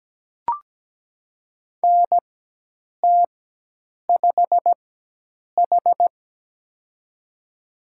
Morse code, heard as a keyed tone, about 17 wpm, spaced as for 8 wpm, 700 Hz. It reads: NT5H
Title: USA Call Signs - 4x Character Spacing - 17wpm